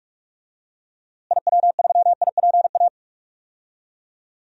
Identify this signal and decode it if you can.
Morse code, heard as a keyed tone, about 41 wpm, 700 Hz. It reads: IW3IPA
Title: DXCC-100 Call Signs - 6 Characters - Repeated 3x - 45wpm